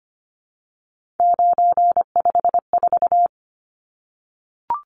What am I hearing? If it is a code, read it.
Morse code, heard as a keyed tone, about 25 wpm, 700 Hz. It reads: 954
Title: Numbers - 3-Digit 25wpm